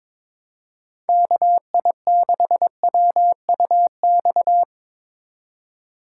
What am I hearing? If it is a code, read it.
Morse code, heard as a keyed tone, about 22 wpm, 700 Hz. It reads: KI6WUX
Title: USA Call Signs - 3x Character Spacing - 22wpm